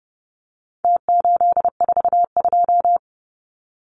Morse code, heard as a keyed tone, about 30 wpm, 700 Hz. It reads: T842
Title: Numbers - Contest Style - 4-Digit 30wpm